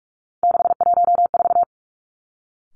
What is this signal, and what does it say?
Morse code, heard as a keyed tone, about 46 wpm, 700 Hz. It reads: 614
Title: Three Challenging Letter-Number Combinations 45wpm